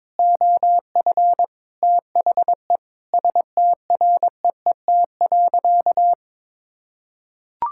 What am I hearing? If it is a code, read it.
Morse code, heard as a keyed tone, about 22 wpm, 700 Hz. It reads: OF THE STREET.
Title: Sentences from Top 1600 Words 22wpm